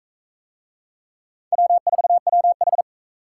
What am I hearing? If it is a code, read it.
Morse code, heard as a keyed tone, about 39 wpm, 700 Hz. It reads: W4WH